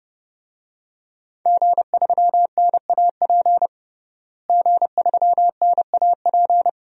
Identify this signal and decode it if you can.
Morse code, heard as a keyed tone, about 30 wpm, 700 Hz. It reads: G3NAP G3NAP
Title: DXCC-100 Call Signs - 5 Characters - Repeated 3x - 30wpm